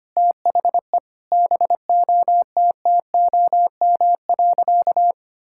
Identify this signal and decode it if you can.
Morse code, heard as a keyed tone, about 25 wpm, 700 Hz. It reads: THE BOTTOM.
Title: Sentences from Top 1000 Words - Review Part 6 - 25wpm